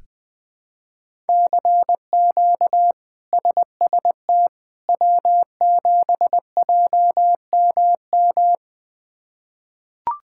Morse code, heard as a keyed tone, about 21 wpm, 700 Hz. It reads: CQ SST W7JMM
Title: K1USN SST QSO Practice - 20wpm